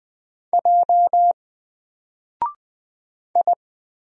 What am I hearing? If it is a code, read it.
Morse code, heard as a keyed tone, about 20 wpm, 700 Hz. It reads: J I
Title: Single Letter-Number 20wpm